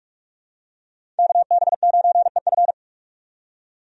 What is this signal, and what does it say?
Morse code, heard as a keyed tone, about 47 wpm, 700 Hz. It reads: KB9EF
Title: USA Call Signs - 5 Characters - 45wpm